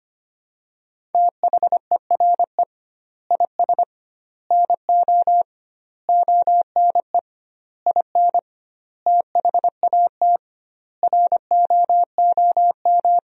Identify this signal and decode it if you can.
Morse code, heard as a keyed tone, about 25 wpm, 700 Hz. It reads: THERE IS NO ONE IN THAT ROOM
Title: 2x Word-Spacing - Sentences from Top 500 Words - 25wpm